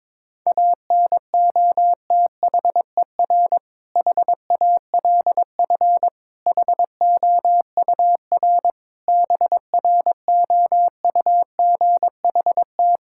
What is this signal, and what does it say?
Morse code, heard as a keyed tone, about 22 wpm, 700 Hz. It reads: ANOTHER HALF HOUR BROUGHT
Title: Sentences from Top 1600 Words 22wpm